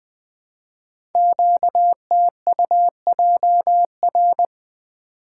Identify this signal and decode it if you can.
Morse code, heard as a keyed tone, about 20 wpm, 700 Hz. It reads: QTUJR
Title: Code Groups - 5 Letters - 20wpm